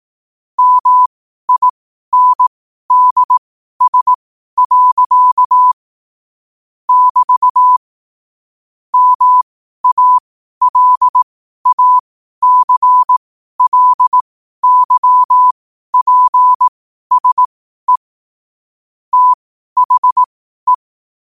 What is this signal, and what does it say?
Morse code, heard as a keyed tone, about 18 wpm, spaced as for 12 wpm, 1000 Hz. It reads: MINDS. <BT> MALACLYPSE THE
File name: qotdfiles / 12wpm